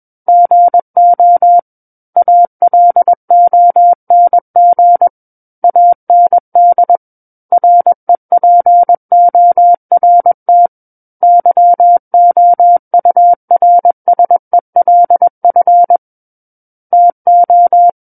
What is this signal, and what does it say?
Morse code, heard as a keyed tone, about 21 wpm, 700 Hz. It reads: GO ALONG AND REPORT YOURSELF TO
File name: Book One / 12-Chapter 12 - 21 WPM